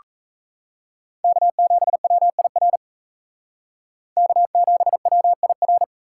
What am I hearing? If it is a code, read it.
Morse code, heard as a keyed tone, about 37 wpm, 700 Hz. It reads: K7WIR K7WIR